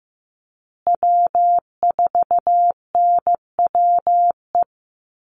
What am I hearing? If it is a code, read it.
Morse code, heard as a keyed tone, about 14 wpm, 700 Hz. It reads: W4NWE
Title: USA Call Signs Repeated 2x 15wpm